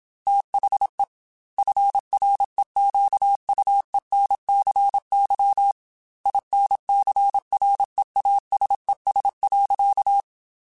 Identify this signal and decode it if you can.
Morse code, heard as a keyed tone, about 27 wpm, 800 Hz. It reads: THE FREQUENCY INCREASES.